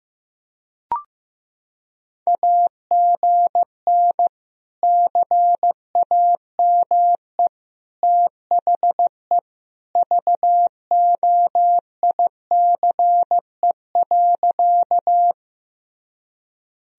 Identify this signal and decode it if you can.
Morse code, heard as a keyed tone, about 16 wpm, 700 Hz. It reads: AGN CAME THE VOICE.
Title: Sentences from Top 500 Words and Abbreviations 15wpm